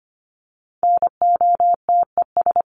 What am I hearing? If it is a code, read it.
Morse code, heard as a keyed tone, about 25 wpm, 700 Hz. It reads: NOTES
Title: ES:S Suffix - Top 100 Words with Base Words 25wpm